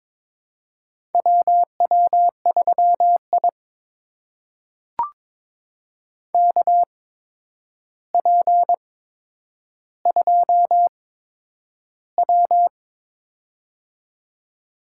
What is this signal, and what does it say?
Morse code, heard as a keyed tone, about 22 wpm, spaced as for 9 wpm, 700 Hz. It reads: WW3I KP2W
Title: USA Call Signs - 4 Characters - 8x Character Spacing - 22wpm